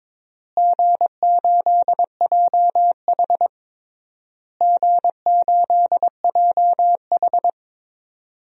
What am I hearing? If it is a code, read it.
Morse code, heard as a keyed tone, about 23 wpm, 700 Hz. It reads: G8JH G8JH
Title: DXCC-100 Call Signs - 4 Characters - Repeated 3x - 22wpm